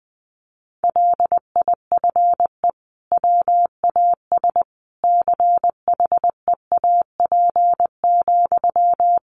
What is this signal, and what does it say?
Morse code, heard as a keyed tone, about 20 wpm, 700 Hz. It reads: LIFE WAS CHEAP,
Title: Sentences from Top 1200 Words 20wpm